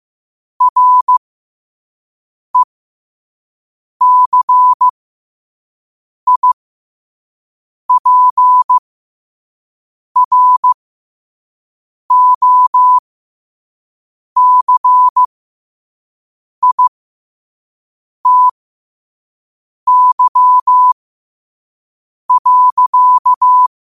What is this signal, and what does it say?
Morse code, heard as a keyed tone, about 15 wpm, spaced as for 5 wpm, 1000 Hz. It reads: RECIPROCITY.
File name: qotdfiles / 5wpm